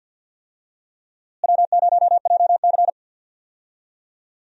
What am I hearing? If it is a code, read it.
Morse code, heard as a keyed tone, about 46 wpm, 700 Hz. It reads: W0JC
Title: USA Call Signs - 4 Characters - Repeated 3x - 50wpm